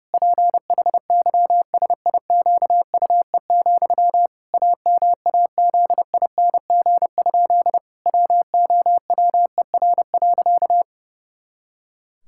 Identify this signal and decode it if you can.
Morse code, heard as a keyed tone, about 30 wpm, 700 Hz. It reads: PHYSIQUE, AMAZING? WOWER.
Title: Pangram Challenge v2 - 30wpm